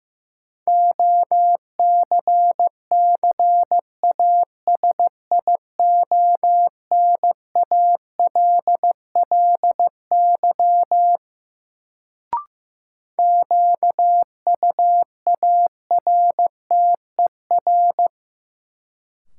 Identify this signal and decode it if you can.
Morse code, heard as a keyed tone, about 15 wpm, 700 Hz. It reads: OCCASIONALLY QUARTER
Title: Sentences from Top 1300 Words - New Word Drill Alt - 15wpm